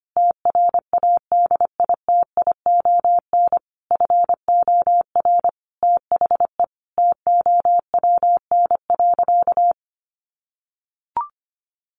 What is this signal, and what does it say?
Morse code, heard as a keyed tone, about 25 wpm, 700 Hz. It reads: TRADITION FOR THE TOWN.
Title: Sentences from Top 1500 Words 25wpm